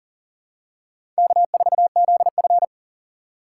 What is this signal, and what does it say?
Morse code, heard as a keyed tone, about 39 wpm, 700 Hz. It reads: K4ZF
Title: USA Call Signs - 4 Characters - 4x Character Spacing - 40wpm